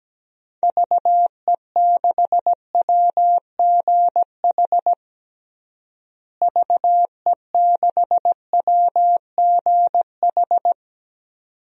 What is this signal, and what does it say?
Morse code, heard as a keyed tone, about 17 wpm, 700 Hz. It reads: VE6WGH VE6WGH
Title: DXCC-100 Call Signs - 6 Characters - Repeated 2x - 17wpm